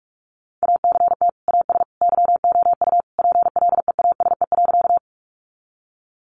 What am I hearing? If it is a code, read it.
Morse code, heard as a keyed tone, about 44 wpm, 700 Hz. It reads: ACT AS YOU PLEASE.
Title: Sentences from Top 1500 Words 45wpm